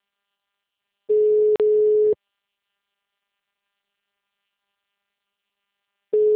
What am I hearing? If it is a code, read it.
Ringback tone (Europe, 425 Hz).
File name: out- -2002- - - .19